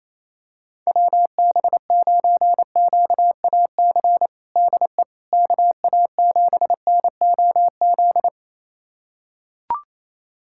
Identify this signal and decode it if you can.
Morse code, heard as a keyed tone, about 28 wpm, 700 Hz. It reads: WB9QAC DE KA7NOZ
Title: USA Callsign DE Callsign - 6 Characters - 28wpm